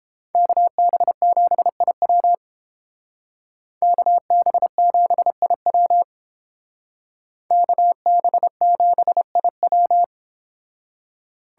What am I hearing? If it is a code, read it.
Morse code, heard as a keyed tone, about 28 wpm, 700 Hz. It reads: KB7IW KB7IW KB7IW